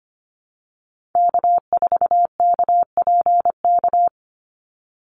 Morse code, heard as a keyed tone, about 25 wpm, 700 Hz. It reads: K4KPK
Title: NA SOTA Activator Callsigns 25wpm